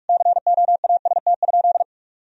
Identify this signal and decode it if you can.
Morse code, heard as a keyed tone, about 41 wpm, 700 Hz. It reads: COAST?
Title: Sentences from Top 1000 Words - Review Part 4 - 45wpm